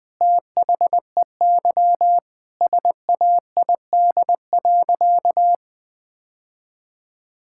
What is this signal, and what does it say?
Morse code, heard as a keyed tone, about 20 wpm, 700 Hz. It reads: THEY SAID.
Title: Sentences with 2-Word Phrases - Spoken Beginning - 20wpm